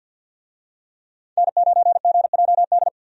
Morse code, heard as a keyed tone, about 50 wpm, 700 Hz. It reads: N9GJD
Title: USA Call Signs - 5 Characters - 50wpm